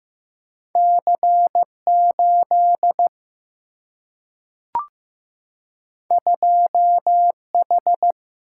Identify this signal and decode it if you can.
Morse code, heard as a keyed tone, about 15 wpm, 700 Hz. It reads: C8 2H
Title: Single-Letter Single-Number 15wpm